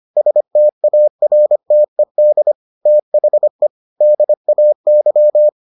Morse code, heard as a keyed tone, about 25 wpm, 600 Hz. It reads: STARTED THE DAY